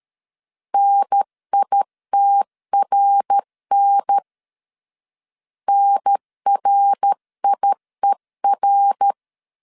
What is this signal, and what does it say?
Morse code, heard as a keyed tone, about 13 wpm, 800 Hz. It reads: NITRN NRIER
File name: CW / DET / 1 3 1500C S A 12wpm 2905